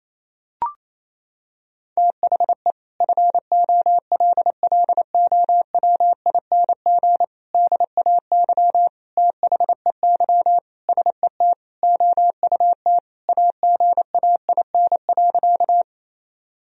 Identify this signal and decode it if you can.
Morse code, heard as a keyed tone, about 28 wpm, 700 Hz. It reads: THE FOLLOWING DAY THEY SET OUT AGAIN.
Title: Sentences from Top 1000 Words - Review Part 4 - 28wpm